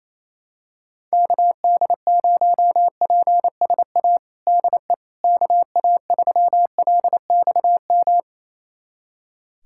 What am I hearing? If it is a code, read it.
Morse code, heard as a keyed tone, about 28 wpm, 700 Hz. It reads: KD0PSA DE KA3LXM
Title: USA Callsign DE Callsign - 6 Characters - 28wpm